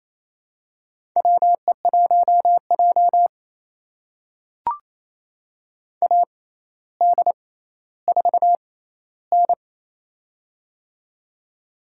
Morse code, heard as a keyed tone, about 28 wpm, spaced as for 14 wpm, 700 Hz. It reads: WE1J AD4N
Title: USA Call Signs - 4 Characters - 6x Character Spacing - 28wpm